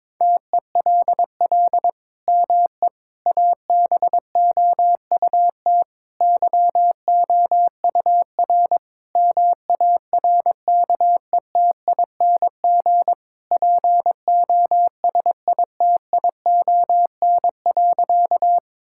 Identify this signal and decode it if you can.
Morse code, heard as a keyed tone, about 22 wpm, 700 Hz. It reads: TELL ME ABOUT YOUR MARKETING POSITION.
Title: Sentences from Top 1000 Words - Review Part 5 - 22wpm